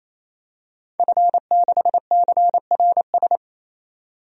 Morse code, heard as a keyed tone, about 28 wpm, 700 Hz. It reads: F6CRS
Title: DXCC-100 Call Signs - 5 Characters - 28wpm